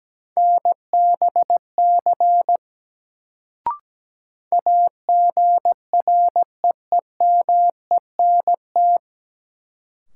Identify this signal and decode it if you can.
Morse code, heard as a keyed tone, about 17 wpm, 700 Hz. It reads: NBC AGREEMENT
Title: Sets of 5 Words Encore - New Word Drill - 17wpm